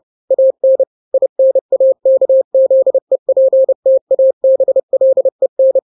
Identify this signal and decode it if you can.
Morse code, heard as a keyed tone, about 30 wpm, 530 Hz. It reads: AN INAKZEPTABLEN
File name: daily / 29wpm